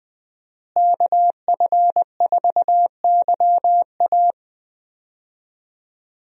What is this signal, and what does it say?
Morse code, heard as a keyed tone, about 20 wpm, 700 Hz. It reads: KF4YA
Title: USA Call Signs - 5 Characters - Repeated 3x - 20wpm